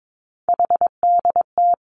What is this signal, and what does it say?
Morse code, heard as a keyed tone, about 22 wpm, 700 Hz. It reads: HDT
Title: Three Challenging Letter-Number Combinations 22wpm